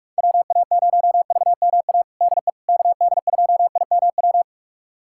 Morse code, heard as a keyed tone, about 46 wpm, 700 Hz. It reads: WA0VMA DE KD2IMW
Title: USA Callsign DE Callsign 45wpm